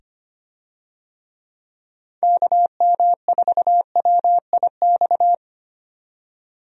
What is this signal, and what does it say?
Morse code, heard as a keyed tone, about 25 wpm, 700 Hz. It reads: KM4WIX
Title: USA Call Signs - 6 Characters - 25wpm